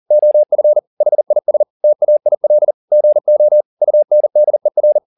Morse code, heard as a keyed tone, about 40 wpm, 600 Hz. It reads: OF HIS TAIL GO UNDER